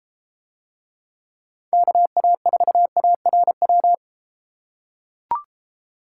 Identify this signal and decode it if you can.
Morse code, heard as a keyed tone, about 31 wpm, 700 Hz. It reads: KA4ARW